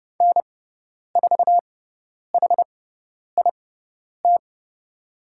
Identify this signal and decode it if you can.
Morse code, heard as a keyed tone, about 30 wpm, spaced as for 10 wpm, 700 Hz. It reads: N4HIT
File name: Lesson 16g - Letter c - Call Signs 30wpm